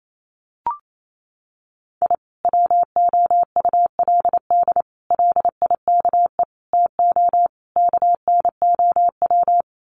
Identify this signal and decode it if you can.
Morse code, heard as a keyed tone, about 28 wpm, 700 Hz. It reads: I WOULD LIKE TO KNOW
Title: Sets of 5 Words Encore 28wpm